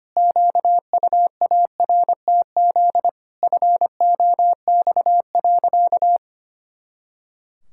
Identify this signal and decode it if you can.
Morse code, heard as a keyed tone, about 25 wpm, 700 Hz. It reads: QUARTZ FOX.
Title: Pangram Challenge v2 - 25wpm